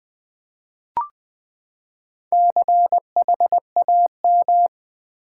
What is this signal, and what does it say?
Morse code, heard as a keyed tone, about 20 wpm, 700 Hz. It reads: CHAM
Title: Build A Word - 4 Letters At A Time - 12-Letter Words - 20wpm